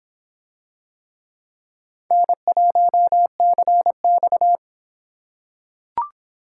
Morse code, heard as a keyed tone, about 24 wpm, 700 Hz. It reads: N1CX